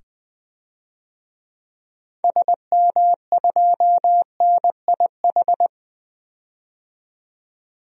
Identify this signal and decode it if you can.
Morse code, heard as a keyed tone, about 20 wpm, 700 Hz. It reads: SM2NIH
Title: DXCC-100 Call Signs - 6 Characters - 20wpm